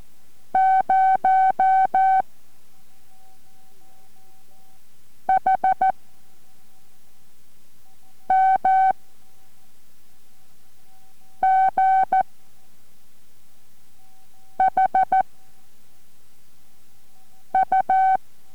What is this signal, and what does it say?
Morse code, heard as a keyed tone, about 14 wpm, 760 Hz. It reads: 0 H M G H U